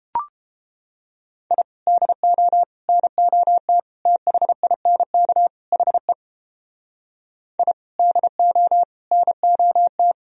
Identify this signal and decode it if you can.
Morse code, heard as a keyed tone, about 33 wpm, 700 Hz. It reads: I DO NOT THINK HE I DO NOT